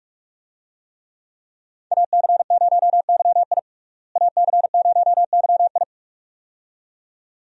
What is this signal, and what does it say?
Morse code, heard as a keyed tone, about 41 wpm, 700 Hz. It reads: AC0YI AC0YI
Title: USA Call Signs - 5 Characters - Repeated 2x - 45wpm